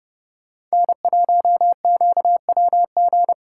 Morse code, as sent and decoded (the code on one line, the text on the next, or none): -. .---- --.- .-- --.
N1QWG